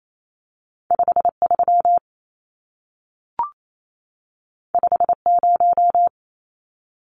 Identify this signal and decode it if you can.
Morse code, heard as a keyed tone, about 28 wpm, 700 Hz. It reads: 53 50
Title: Numbers - 2-Digit 28wpm